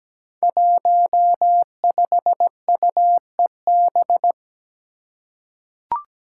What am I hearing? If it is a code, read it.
Morse code, heard as a keyed tone, about 17 wpm, 700 Hz. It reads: 15UEB
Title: Five Challenging Letter-Number Combinations 17wpm